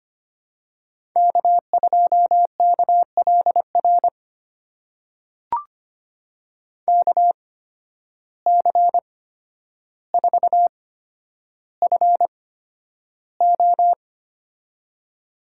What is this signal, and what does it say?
Morse code, heard as a keyed tone, about 25 wpm, spaced as for 11 wpm, 700 Hz. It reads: K2KLR KC4FO
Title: USA Call Signs - 5 Characters - 8x Character Spacing - 25wpm